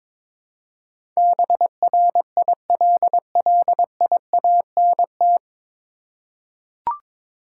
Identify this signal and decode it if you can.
Morse code, heard as a keyed tone, about 22 wpm, 700 Hz. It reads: BRILLIANT
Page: Morse Code Ninja practice 22wpm